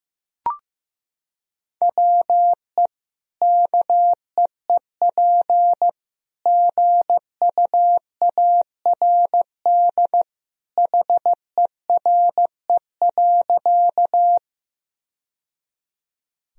Morse code, heard as a keyed tone, about 15 wpm, 700 Hz. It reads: WE KEEP GUARD HERE.
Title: Sentences from Top 1000 Words - Review Part 4 - 15wpm